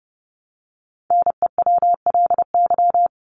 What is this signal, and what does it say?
Morse code, heard as a keyed tone, about 30 wpm, 700 Hz. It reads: NEWLY
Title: LY Suffix - Top 100 Words with Base Words 30wpm